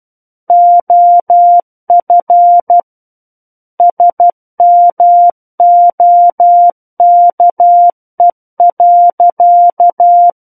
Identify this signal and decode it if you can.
Morse code, heard as a keyed tone, about 12 wpm, 700 Hz. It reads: OF SMOKE.
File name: Book One / 03-Chapter 3 - 12 WPM